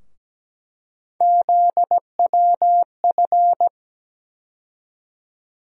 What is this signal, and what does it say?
Morse code, heard as a keyed tone, about 17 wpm, 700 Hz. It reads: ZWF
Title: USA Call Sign Postfixes - 3 Letters 17wpm